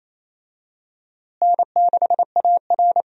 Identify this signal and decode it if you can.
Morse code, heard as a keyed tone, about 28 wpm, 700 Hz. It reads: N6AR
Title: USA Call Signs - 4 Characters - Repeated 2x - 28wpm